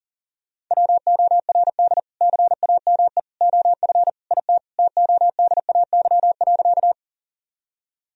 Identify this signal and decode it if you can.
Morse code, heard as a keyed tone, about 37 wpm, 700 Hz. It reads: WORD CAME OF IT TODAY.
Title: Sentences from Top 1000 Words - Review Part 4 - 40wpm